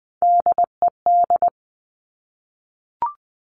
Morse code, heard as a keyed tone, about 20 wpm, 700 Hz. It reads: DED
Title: Top Three Letter Combinations in Words 20wpm